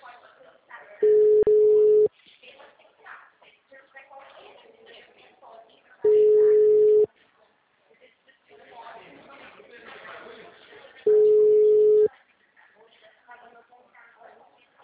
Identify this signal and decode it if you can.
Ringback tone (Europe, 425 Hz), with speech.